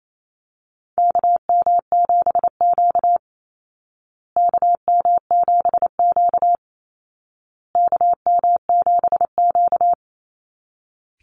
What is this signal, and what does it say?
Morse code, heard as a keyed tone, about 28 wpm, 700 Hz. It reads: KM7Q KM7Q KM7Q
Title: Call Signs Repeated 2x - 28wpm